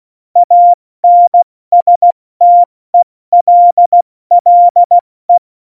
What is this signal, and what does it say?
Morse code, heard as a keyed tone, about 16 wpm, spaced as for 14 wpm, 700 Hz. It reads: ANSTELLE
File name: Morsekurs / Witze / Witz 06. November 2018, 70 F80